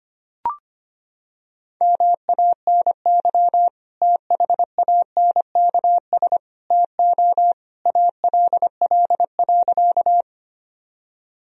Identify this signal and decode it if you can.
Morse code, heard as a keyed tone, about 25 wpm, 700 Hz. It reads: MANY THANKS TO ALL.
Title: Sentences from Top 1000 Words - Review Part 3 - 25wpm